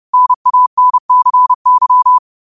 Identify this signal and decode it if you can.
Morse code, heard as a keyed tone, about 30 wpm, 1000 Hz. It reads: NANCY